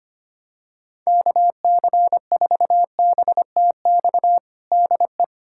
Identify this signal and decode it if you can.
Morse code, heard as a keyed tone, about 25 wpm, 700 Hz. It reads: KC4BTX DE
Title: USA Callsign DE Callsign - 6 Characters - 25wpm